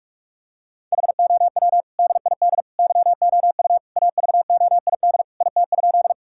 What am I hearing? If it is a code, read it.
Morse code, heard as a keyed tone, about 46 wpm, 700 Hz. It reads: HOW DID YOU AVOID IT?
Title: Sentences from Top 1000 Words - Review Part 6 - 45wpm